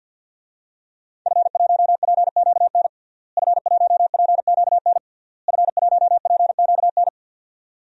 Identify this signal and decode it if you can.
Morse code, heard as a keyed tone, about 51 wpm, 700 Hz. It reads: F1PQN F1PQN F1PQN
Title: DXCC-100 Call Signs - 5 Characters - Repeated 3x - 50wpm